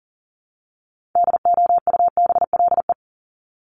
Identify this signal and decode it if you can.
Morse code, heard as a keyed tone, about 40 wpm, 700 Hz. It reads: DOUBLE
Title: Top 1251-1500 Words 40wpm